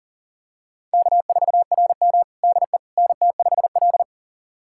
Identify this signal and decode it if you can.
Morse code, heard as a keyed tone, about 40 wpm, 700 Hz. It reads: K4RM DE NT5L
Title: USA Callsign DE Callsign - 4 Characters - 40wpm